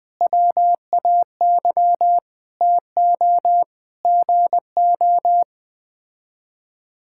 Morse code, heard as a keyed tone, about 20 wpm, 700 Hz. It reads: WAY TO GO
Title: Sets of 9 Words - 20wpm